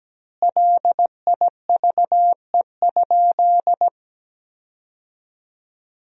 Morse code, heard as a keyed tone, about 17 wpm, 700 Hz. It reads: LIVE?
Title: 4x Word-Spacing - Sentences from Top 200 Words - 17wpm